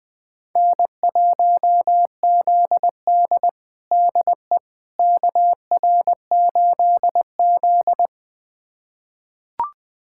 Morse code, heard as a keyed tone, about 20 wpm, 700 Hz. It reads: N1ZD DE KR8Z
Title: USA Callsign DE Callsign - 4 Characters - 20wpm